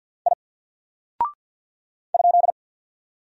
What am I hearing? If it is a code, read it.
Morse code, heard as a keyed tone, about 48 wpm, 700 Hz. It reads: I ?
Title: Single Character (Letter:Number:Punctuation:Prosign) 50wpm